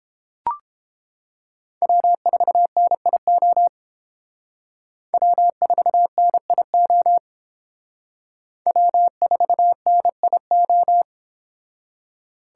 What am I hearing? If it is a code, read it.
Morse code, heard as a keyed tone, about 29 wpm, 700 Hz. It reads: W4NIO W4NIO W4NIO